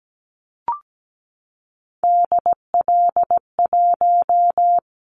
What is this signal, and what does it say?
Morse code, heard as a keyed tone, about 17 wpm, 700 Hz. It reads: DL1
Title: Callsign Prefixes for Top 25 DXCC - 17wpm